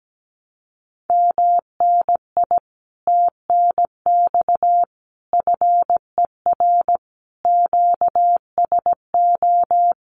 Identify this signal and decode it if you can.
Morse code, heard as a keyed tone, about 17 wpm, 700 Hz. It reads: MNI TNX FER QSO
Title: Common QSO Phrases Encore 17wpm